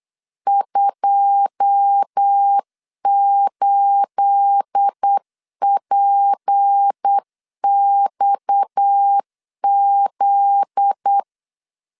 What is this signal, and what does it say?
Morse code, heard as a keyed tone, about 8 wpm, 800 Hz. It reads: 28PXZ
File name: LEZIONI / 9 28 8wpm 3512